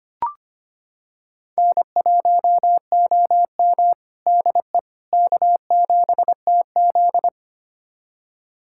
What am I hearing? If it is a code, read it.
Morse code, heard as a keyed tone, about 25 wpm, 700 Hz. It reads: N1OM DE K7TZ
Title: USA Callsign DE Callsign - 4 Characters - 25wpm